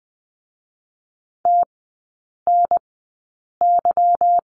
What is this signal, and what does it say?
Morse code, heard as a keyed tone, about 20 wpm, 700 Hz. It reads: T N Y
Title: 145 Lesson 24e - Letter y - Three Characters 20wpm